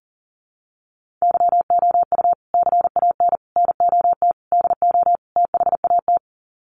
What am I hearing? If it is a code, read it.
Morse code, heard as a keyed tone, about 40 wpm, 700 Hz. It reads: YOU CAN NOT DO THAT
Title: Sets of 5 Words 40wpm